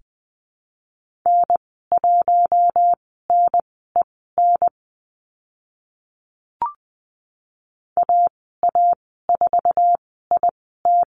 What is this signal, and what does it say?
Morse code, heard as a keyed tone, about 20 wpm, spaced as for 14 wpm, 700 Hz. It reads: N1NEN AA4IT
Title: 048 Lesson 9g - Number 4 - Call Signs 20wpm